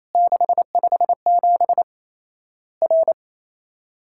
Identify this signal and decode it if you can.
Morse code, heard as a keyed tone, about 27 wpm, 700 Hz. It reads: 657 R
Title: Worked All Europe DX Contest - QTC Exchange - 28wpm